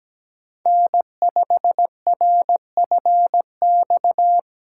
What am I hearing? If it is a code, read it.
Morse code, heard as a keyed tone, about 17 wpm, 700 Hz. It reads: N5RFX
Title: USA Call Signs - 5 Characters - 4x Character Spacing - 17wpm